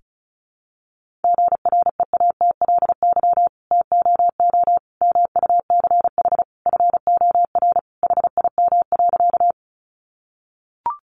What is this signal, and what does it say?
Morse code, heard as a keyed tone, about 34 wpm, 700 Hz. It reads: GREATLY TOO MUCH FOR HIM.
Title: Sentences from Top 1400 Words 35wpm